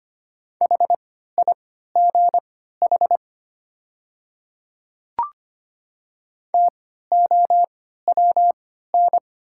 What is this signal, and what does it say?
Morse code, heard as a keyed tone, about 25 wpm, spaced as for 14 wpm, 700 Hz. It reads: HIGH TOWN
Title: 3x Character Spacing - Top 500 Words - 25wpm